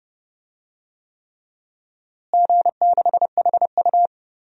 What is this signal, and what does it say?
Morse code, heard as a keyed tone, about 30 wpm, 700 Hz. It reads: G6HU
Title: DXCC-100 Call Signs - 4 Characters - Repeated 3x - 30wpm